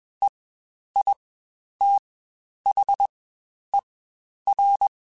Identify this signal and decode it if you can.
Morse code, heard as a keyed tone, about 21 wpm, spaced as for 10 wpm, 800 Hz. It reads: EITHER
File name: Spaced / Chuckles Spaced-mp3s / Squeaky sp @20